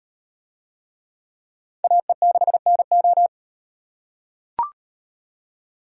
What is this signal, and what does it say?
Morse code, heard as a keyed tone, about 38 wpm, 700 Hz. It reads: AE6NO